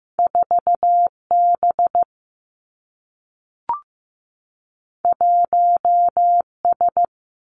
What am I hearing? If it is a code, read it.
Morse code, heard as a keyed tone, about 15 wpm, 700 Hz. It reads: 4 B 1 S
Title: Two Challenging Letter-Number Combinations 15wpm